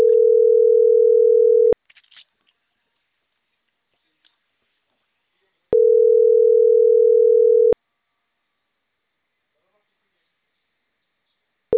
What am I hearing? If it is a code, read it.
Ringback tone (North America), 440 + 480 Hz.
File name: CENTRAL CAEA / out agcaea10